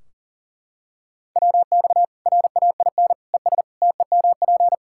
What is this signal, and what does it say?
Morse code, heard as a keyed tone, about 40 wpm, 700 Hz. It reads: WX RAIN ES TEMP
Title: Practice QSO - Exchange 2 - Info - 40wpm